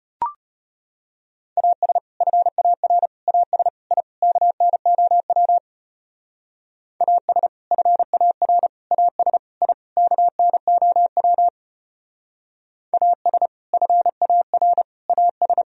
Morse code, heard as a keyed tone, about 34 wpm, 700 Hz. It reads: AS FAR AS I KNOW AS FAR AS I KNOW AS FAR AS